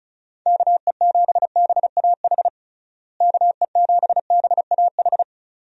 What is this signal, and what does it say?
Morse code, heard as a keyed tone, about 34 wpm, 700 Hz. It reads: KE7BAH KE7BAH
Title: USA Call Signs - 6 Characters - Repeated 2x - 35wpm